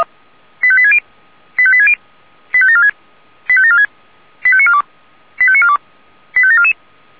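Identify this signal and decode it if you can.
Five-tone selective call (ZVEI), heard as quick runs of five tones, tones of about 70 ms. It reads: 85860 85860 85746 85746 85942 85942 85744
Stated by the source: ZVEI five-tone selective call 85724 received on SDR.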